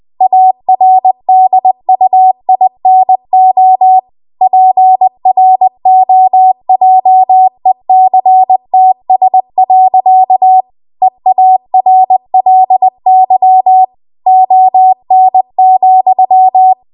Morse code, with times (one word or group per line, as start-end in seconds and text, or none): 0.2-4.0 ARDUINO
4.4-10.6 PROJECTS.
11.0-13.8 EARLY
14.3-16.7 ON,